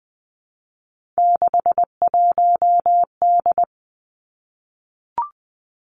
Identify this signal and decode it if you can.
Morse code, heard as a keyed tone, about 20 wpm, 700 Hz. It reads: 61D
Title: Three Challenging Letter-Number Combinations 20wpm